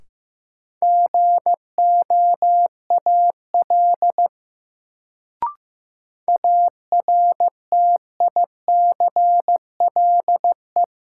Morse code, heard as a keyed tone, about 15 wpm, 700 Hz. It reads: GOAL ARTICLE
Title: Sentences from Top 1000 Words - Review Part 5 - New Word Drill - 15wpm